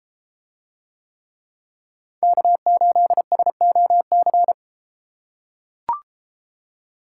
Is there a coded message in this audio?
Yes: Morse code from 2.2 to 4.5 s, heard as a keyed tone, about 30 wpm, 700 Hz. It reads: K8SOC